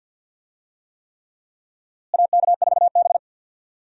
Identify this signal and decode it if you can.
Morse code, heard as a keyed tone, about 50 wpm, 700 Hz. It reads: AK4B
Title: USA Call Signs - 4 Characters - Repeated 3x - 50wpm